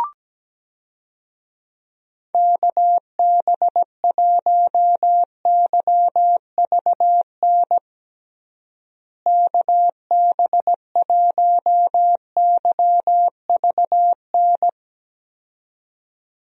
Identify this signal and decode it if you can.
Morse code, heard as a keyed tone, about 17 wpm, 700 Hz. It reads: KB1YVN KB1YVN
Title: USA Call Signs - 6 Characters - Repeated 2x - 17wpm